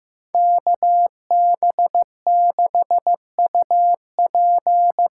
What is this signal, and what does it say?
Morse code, heard as a keyed tone, about 15 wpm, 700 Hz. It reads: KB6UP
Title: USA Call Signs - 6 Characters - 6x Character Spacing - 15wpm